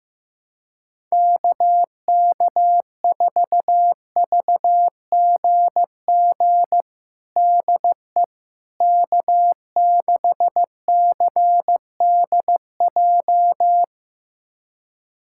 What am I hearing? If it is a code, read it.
Morse code, heard as a keyed tone, about 16 wpm, 700 Hz. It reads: KK4VGG DE K6CDJ
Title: USA Callsign DE Callsign 15wpm